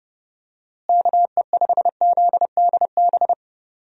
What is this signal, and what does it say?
Morse code, heard as a keyed tone, about 32 wpm, 700 Hz. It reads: KE5ZDB
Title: USA Call Signs - 6 Characters - Repeated 2x - 30wpm